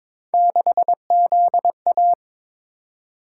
Morse code, heard as a keyed tone, about 22 wpm, 700 Hz. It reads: 6ZA
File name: practice / 22wpm / Three Letter-Number Combinations - 22wpm